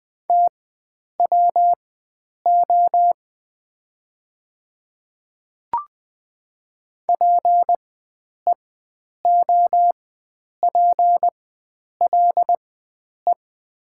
Morse code, heard as a keyed tone, about 20 wpm, spaced as for 9 wpm, 700 Hz. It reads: TWO PEOPLE
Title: 4x Character Spacing - Top 100 Words - 20wpm